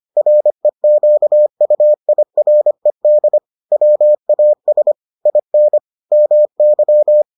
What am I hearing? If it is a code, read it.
Morse code, heard as a keyed tone, about 25 wpm, 600 Hz. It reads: REQUIRED WAS IN MY